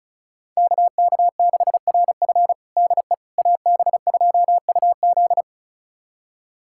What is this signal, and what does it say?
Morse code, heard as a keyed tone, about 36 wpm, 700 Hz. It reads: KK6RF DE AB2UZ
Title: USA Callsign DE Callsign - 5 Characters - 35wpm